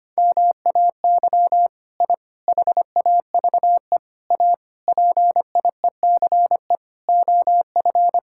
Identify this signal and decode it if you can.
Morse code, heard as a keyed tone, about 25 wpm, 700 Hz. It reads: MAY I HAVE A PIECE OF
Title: Sentences from Top 1000 Words - Review Part 5 - 25wpm